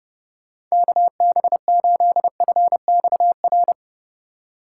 Morse code, heard as a keyed tone, about 37 wpm, 700 Hz. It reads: KB8FXR
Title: Speed Racing - USA 2x3 Call Signs 30wpm